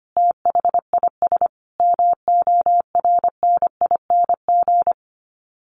Morse code, heard as a keyed tone, about 25 wpm, 700 Hz. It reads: THIS MORNING
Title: Sets of 5 Words Encore 25wpm